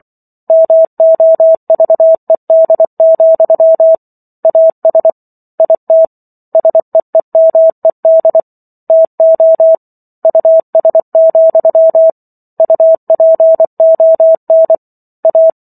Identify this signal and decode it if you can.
Morse code, heard as a keyed tone, about 24 wpm, 650 Hz. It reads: MOVED, AS IT SEEMED TO US, UPON A